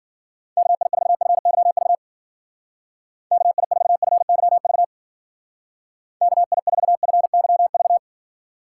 Morse code, heard as a keyed tone, about 50 wpm, 700 Hz. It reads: KI4FYV KI4FYV KI4FYV
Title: Speed Racing - USA Call Signs 40wpm